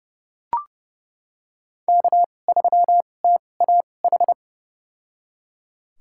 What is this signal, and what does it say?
Morse code, heard as a keyed tone, about 30 wpm, spaced as for 22 wpm, 700 Hz. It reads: K3TAH
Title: USA Call Signs - 2x Character Spacing - 30wpm